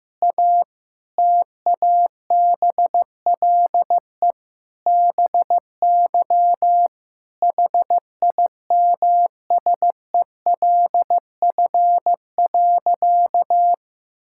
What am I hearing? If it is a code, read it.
Morse code, heard as a keyed tone, about 15 wpm, 700 Hz. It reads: A TABLE BY HIMSELF.
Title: Sentences from Top 1600 Words 15wpm